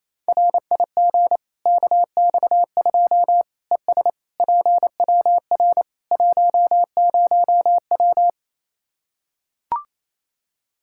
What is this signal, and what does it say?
Morse code, heard as a keyed tone, about 28 wpm, 700 Hz. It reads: RIG KX2 ES PWR 10W
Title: Practice QSO - Exchange 2 - Info - 28wpm